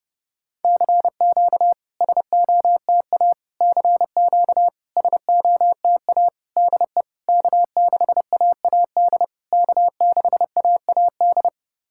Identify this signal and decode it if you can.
Morse code, heard as a keyed tone, about 32 wpm, 690 Hz. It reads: CQ SOTA CQ SOTA DE K6AAD K6AAD
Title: SOTA QSO Practice 30wpm